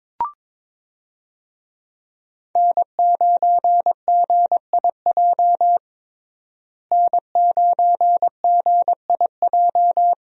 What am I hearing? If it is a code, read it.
Morse code, heard as a keyed tone, about 22 wpm, 700 Hz. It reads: N9GIJ N9GIJ
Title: USA Call Signs - 5 Characters - Repeated 2x - 22wpm